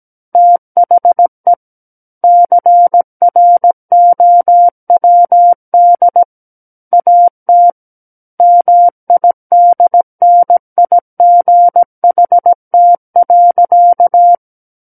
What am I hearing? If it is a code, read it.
Morse code, heard as a keyed tone, about 17 wpm, 700 Hz. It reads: THE CROWD AT MIDNIGHT.
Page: SKCC Learning Center - Morse Code Practice Files